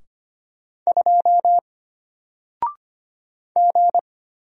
Morse code, heard as a keyed tone, about 25 wpm, 700 Hz. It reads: 2 G
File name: Single-Letter-Number-25wpm